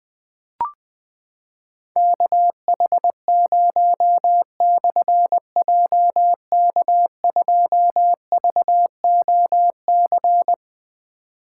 Morse code, heard as a keyed tone, about 20 wpm, 700 Hz. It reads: KH0/JK2VOC
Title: Difficult Call Signs 20wpm